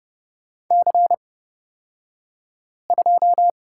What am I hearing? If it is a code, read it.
Morse code, heard as a keyed tone, about 30 wpm, spaced as for 5 wpm, 700 Hz. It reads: C2
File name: Lesson 38d - Letter q - Two Characters 30wpm